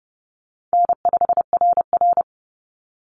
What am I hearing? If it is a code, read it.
Morse code, heard as a keyed tone, about 30 wpm, 700 Hz. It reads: N5RR
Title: Easy Call Signs Repeated 2x - 30wpm